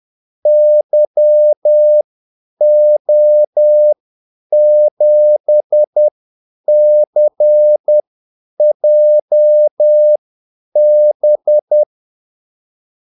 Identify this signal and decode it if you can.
Morse code, heard as a keyed tone, about 10 wpm, spaced as for 8 wpm, 600 Hz. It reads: YO7CJB
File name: Callsigns-10-8wpm